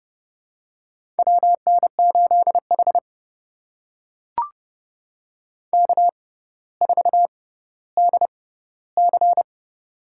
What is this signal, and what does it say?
Morse code, heard as a keyed tone, about 31 wpm, spaced as for 16 wpm, 700 Hz. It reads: WN8H K4DC
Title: USA Call Signs - 4 Characters - 6x Character Spacing - 30wpm